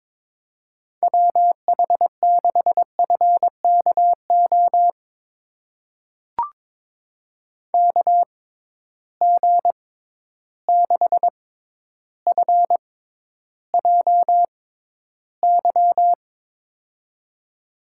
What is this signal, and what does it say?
Morse code, heard as a keyed tone, about 22 wpm, spaced as for 11 wpm, 700 Hz. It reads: WH6FKO KG6FJY
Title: USA Call Signs - 6 Characters - 6x Character Spacing - 22wpm